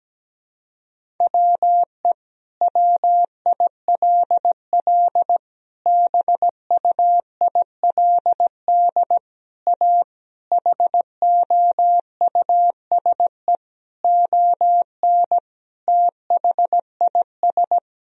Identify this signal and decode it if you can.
Morse code, heard as a keyed tone, about 17 wpm, 700 Hz. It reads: WE WILL BUILD A HOUSE ON THIS
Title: Sentences from Top 1000 Words - Review Part 4 - 17wpm